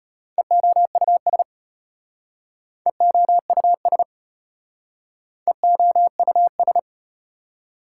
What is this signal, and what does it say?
Morse code, heard as a keyed tone, about 34 wpm, 700 Hz. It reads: EOUS EOUS EOUS